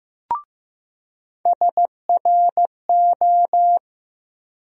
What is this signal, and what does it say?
Morse code, heard as a keyed tone, about 15 wpm, 700 Hz. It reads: SRO
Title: USA Call Sign Postfixes - 3 Letters 15wpm